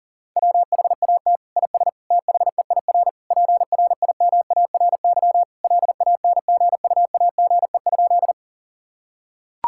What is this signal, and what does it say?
Morse code, heard as a keyed tone, about 40 wpm, 700 Hz. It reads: WHAT IS THEIR PRIMARY LANGUAGE?
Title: Sentences from Top 1000 Words - Review Part 5 - 40wpm